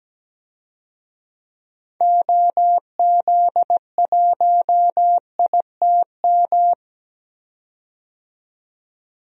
Morse code, heard as a keyed tone, about 17 wpm, 700 Hz. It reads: OZ1ITM
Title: DXCC-100 Call Signs - 6 Characters - 17wpm